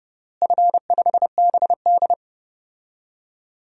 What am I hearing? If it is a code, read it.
Morse code, heard as a keyed tone, about 29 wpm, 700 Hz. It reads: F5BD
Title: DXCC-100 Call Signs - 4 Characters - Repeated 3x - 30wpm